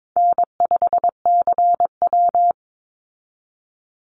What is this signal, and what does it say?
Morse code, heard as a keyed tone, about 22 wpm, 690 Hz. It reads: N5CW
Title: Call Signs Repeated 3x - 22wpm